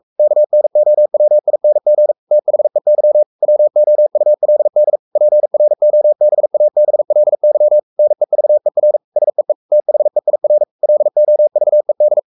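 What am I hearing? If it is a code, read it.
Morse code, heard as a keyed tone, about 44 wpm, 600 Hz. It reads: KNOWING THEY WOULD PROBABLY NEVER SEE THEIR LOVED
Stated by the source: Morse code at 43 wpm sending Ken Follett's novel Never in English, from CW med Gnister.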